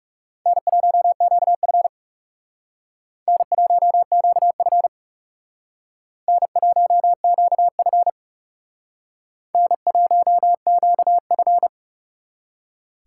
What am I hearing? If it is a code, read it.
Morse code, heard as a keyed tone, about 39 wpm, 700 Hz. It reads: N1QF N1QF N1QF N1QF